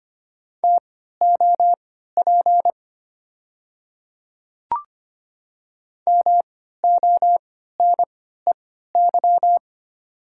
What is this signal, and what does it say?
Morse code, heard as a keyed tone, about 25 wpm, spaced as for 14 wpm, 690 Hz. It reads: TOP MONEY
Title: 3x Character Spacing - Top 500 Words - 25wpm